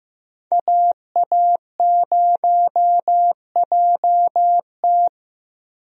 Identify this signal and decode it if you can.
Morse code, heard as a keyed tone, about 15 wpm, 700 Hz. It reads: AA0JT
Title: USA Call Signs - 5 Characters - Repeated 2x - 15wpm